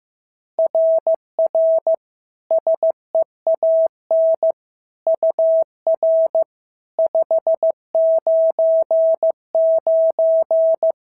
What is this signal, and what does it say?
Morse code, heard as a keyed tone, about 15 wpm, 650 Hz. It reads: RR SEAN UR 599
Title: SOTA QSO Practice 15wpm